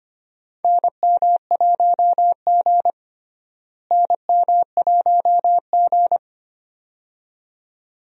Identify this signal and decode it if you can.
Morse code, heard as a keyed tone, about 25 wpm, 700 Hz. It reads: NM1G NM1G
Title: USA Call Signs - 4 Characters - Repeated 2x - 25wpm